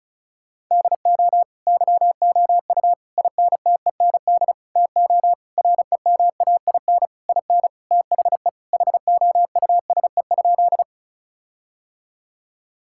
Morse code, heard as a keyed tone, about 34 wpm, 700 Hz. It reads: DO YOU INTEND TO REMAIN IN THE HOUSE?
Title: Sentences from Top 1600 Words 35wpm